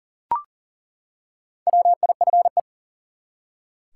Morse code, heard as a keyed tone, about 37 wpm, 700 Hz. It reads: WIFE
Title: Four Letter Words - 40wpm